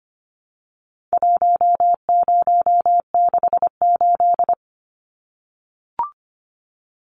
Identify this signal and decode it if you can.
Morse code, heard as a keyed tone, about 25 wpm, 700 Hz. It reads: 1068
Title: Numbers - 4-Digit 25wpm